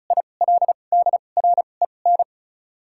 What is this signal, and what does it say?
Morse code, heard as a keyed tone, about 36 wpm, spaced as for 25 wpm, 700 Hz. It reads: ILDREN
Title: 2x Character Spacing - Top 300 Words - 35wpm